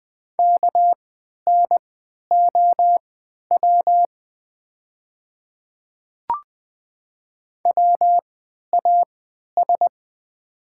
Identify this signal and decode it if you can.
Morse code, heard as a keyed tone, about 20 wpm, spaced as for 11 wpm, 690 Hz. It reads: KNOW WAS
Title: 3x Character Spacing - Top 200 Words - 20wpm